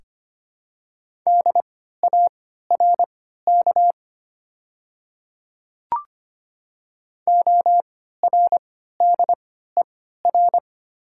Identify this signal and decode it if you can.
Morse code, heard as a keyed tone, about 25 wpm, spaced as for 14 wpm, 700 Hz. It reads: DARK ORDER
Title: 3x Character Spacing - Top 500 Words - 25wpm